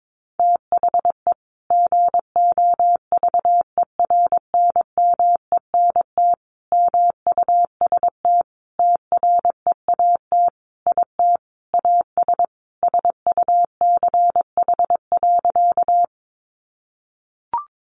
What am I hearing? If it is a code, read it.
Morse code, heard as a keyed tone, about 22 wpm, 700 Hz. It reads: THE GOVERNMENT MUST TREAT IT AS SUCH.
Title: Sentences from Top 1300 Words 22wpm